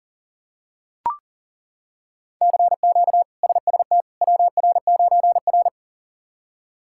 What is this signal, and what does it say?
Morse code, heard as a keyed tone, about 40 wpm, 700 Hz. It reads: CQ SST WR9R
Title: K1USN SST QSO Practice - 40wpm